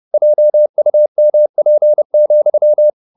Morse code, heard as a keyed tone, about 30 wpm, 600 Hz. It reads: JUMP,